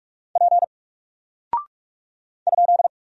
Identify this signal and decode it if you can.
Morse code, heard as a keyed tone, about 44 wpm, 700 Hz. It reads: P ?
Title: Single Character (Letter:Number:Punctuation:Prosign) Alt 45wpm